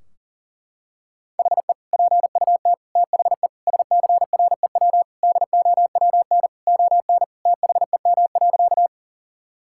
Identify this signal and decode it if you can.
Morse code, heard as a keyed tone, about 37 wpm, 700 Hz. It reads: HE PUT THE SCREW DOWN ON THEM.
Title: Sentences from Top 1600 Words 40wpm